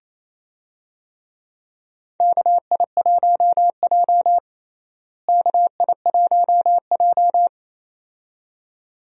Morse code, heard as a keyed tone, about 28 wpm, 700 Hz. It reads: KI1J KI1J
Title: USA Call Signs - 4 Characters - Repeated 2x - 28wpm